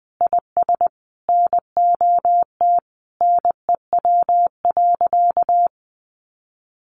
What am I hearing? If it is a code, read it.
Morse code, heard as a keyed tone, about 20 wpm, 700 Hz. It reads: IS NOT NEW.
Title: Sentences from Top 1100 Words 20wpm